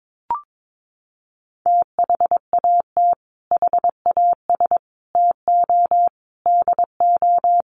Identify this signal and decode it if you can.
Morse code, heard as a keyed tone, about 22 wpm, 700 Hz. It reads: THAT HAS TO DO
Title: Sets of 5 Words Encore2 22wpm